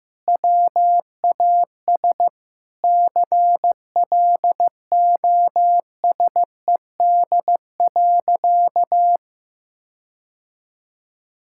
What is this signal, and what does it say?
Morse code, heard as a keyed tone, about 15 wpm, 700 Hz. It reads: WAS CLOSED.
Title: Sentences from Top 1600 Words 15wpm